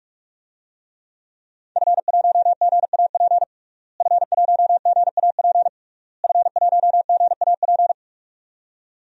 Morse code, heard as a keyed tone, about 46 wpm, 700 Hz. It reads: F1GAP F1GAP F1GAP
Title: DXCC-100 Call Signs - 5 Characters - Repeated 3x - 45wpm